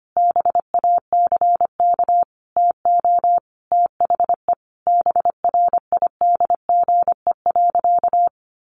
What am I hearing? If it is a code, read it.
Morse code, heard as a keyed tone, about 25 wpm, 700 Hz. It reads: BACK TO THE BRIDGE.
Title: Sentences from Top 1200 Words 25wpm